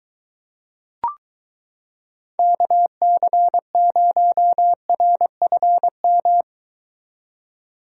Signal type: Morse code

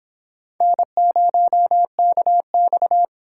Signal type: Morse code

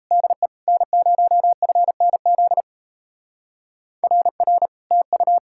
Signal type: Morse code

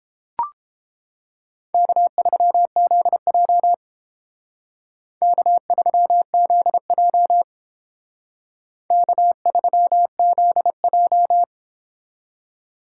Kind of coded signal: Morse code

